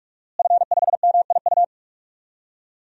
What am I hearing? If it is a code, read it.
Morse code, heard as a keyed tone, about 46 wpm, 700 Hz. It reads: F5MIU
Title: DXCC-100 Call Signs - 5 Characters - Repeated 2x - 45wpm